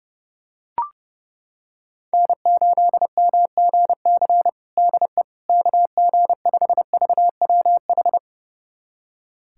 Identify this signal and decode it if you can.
Morse code, heard as a keyed tone, about 30 wpm, 700 Hz. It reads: N8MGC DE KG5VWH
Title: USA Callsign DE Callsign 30wpm